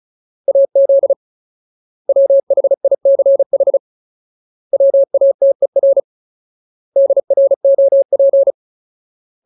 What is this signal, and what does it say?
Morse code, heard as a keyed tone, about 34 wpm, 550 Hz. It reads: AZ WHICH WATER DROP
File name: Ultimate Word List / 35WPM Ultimate